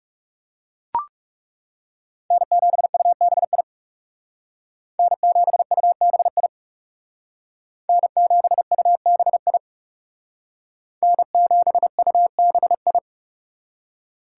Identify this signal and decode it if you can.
Morse code, heard as a keyed tone, about 40 wpm, 700 Hz. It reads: N7UBI N7UBI N7UBI N7UBI